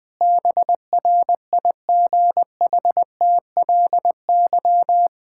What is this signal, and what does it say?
Morse code, heard as a keyed tone, about 20 wpm, 700 Hz. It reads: BRIGHTLY
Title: Sentences with 2-Word Phrases - Spoken Beginning - 20wpm